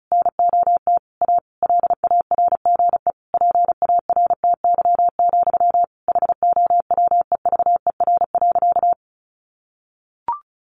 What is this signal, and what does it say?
Morse code, heard as a keyed tone, about 34 wpm, 700 Hz. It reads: NOT A LARGE PARTY, HOWEVER.
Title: Sentences from Top 1400 Words 35wpm